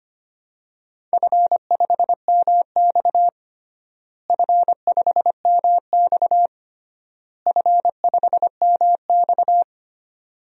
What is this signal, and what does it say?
Morse code, heard as a keyed tone, about 25 wpm, 700 Hz. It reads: F5MX F5MX F5MX
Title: DXCC-100 Call Signs - 4 Characters - Repeated 3x - 25wpm